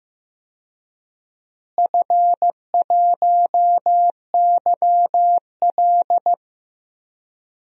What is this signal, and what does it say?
Morse code, heard as a keyed tone, about 15 wpm, 700 Hz. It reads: F1YL
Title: DXCC-100 Call Signs - 4 Characters - Repeated 3x - 15wpm